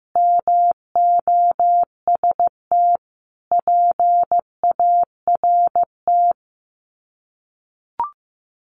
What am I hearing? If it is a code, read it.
Morse code, heard as a keyed tone, about 15 wpm, 700 Hz. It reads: MOST PART
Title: Sets of 4 Words 15wpm